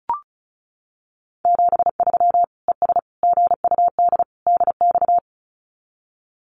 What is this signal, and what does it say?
Morse code, heard as a keyed tone, about 34 wpm, 700 Hz. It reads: 73 ES GUD DX
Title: Common QSO Phrases 35wpm